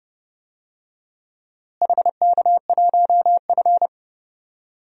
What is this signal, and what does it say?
Morse code, heard as a keyed tone, about 30 wpm, 700 Hz. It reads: HK1F
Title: DXCC-100 Call Signs - 4 Characters - Repeated 3x - 30wpm